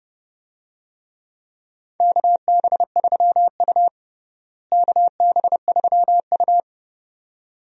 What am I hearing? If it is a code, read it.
Morse code, heard as a keyed tone, about 31 wpm, 700 Hz. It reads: KB3U KB3U
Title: USA Call Signs - 4 Characters - Repeated 2x - 30wpm